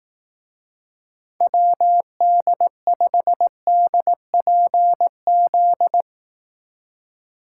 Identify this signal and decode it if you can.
Morse code, heard as a keyed tone, about 18 wpm, 700 Hz. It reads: WD5DPZ
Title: Speed Racing - USA 2x3 Call Signs 15wpm